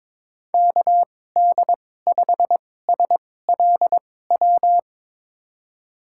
Morse code, heard as a keyed tone, about 22 wpm, spaced as for 16 wpm, 700 Hz. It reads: KD5SLW
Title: USA Call Signs - 2x Character Spacing - 22wpm